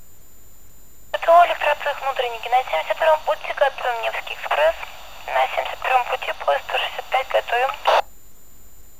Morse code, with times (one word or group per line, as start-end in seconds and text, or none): none